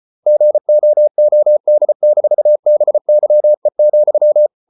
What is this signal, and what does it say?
Morse code, heard as a keyed tone, about 34 wpm, 600 Hz. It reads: GOOD-BYE,